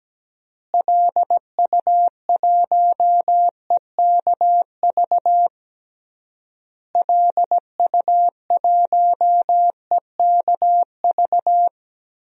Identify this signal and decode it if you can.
Morse code, heard as a keyed tone, about 17 wpm, 700 Hz. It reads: LU1EKV LU1EKV
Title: DXCC-100 Call Signs - 6 Characters - Repeated 3x - 17wpm